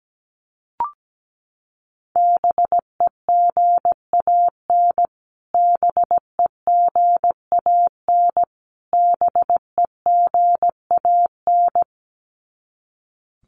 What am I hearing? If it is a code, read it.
Morse code, heard as a keyed tone, about 17 wpm, 700 Hz. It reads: BEGAN BEGAN BEGAN
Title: Top 300 Words Repeated 17wpm